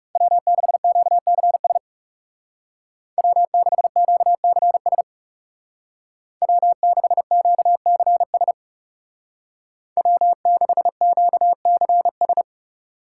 Morse code, heard as a keyed tone, about 40 wpm, 700 Hz. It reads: W6QCS W6QCS W6QCS W6QCS